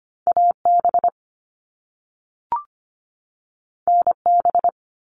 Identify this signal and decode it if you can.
Morse code, heard as a keyed tone, about 25 wpm, 700 Hz. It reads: AB NB
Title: Canadian Provinces and Territories 25wpm